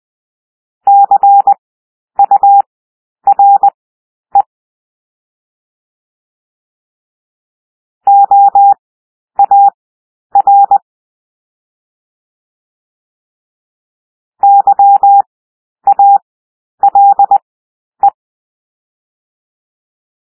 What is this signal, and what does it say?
Morse code, heard as a keyed tone, about 20 wpm, spaced as for 10 wpm, 800 Hz. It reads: CURE OAR YALE
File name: cwmp3 / W1 SOUND WORDS 20 wpm